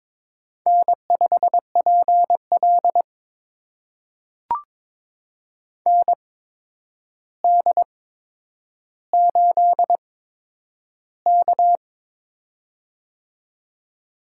Morse code, heard as a keyed tone, about 22 wpm, spaced as for 9 wpm, 700 Hz. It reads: N5PL ND8K
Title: USA Call Signs - 4 Characters - 8x Character Spacing - 22wpm